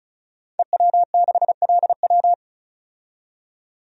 Morse code, heard as a keyed tone, about 36 wpm, 700 Hz. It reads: EW6LW
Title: DXCC-100 Call Signs - 5 Characters - 35wpm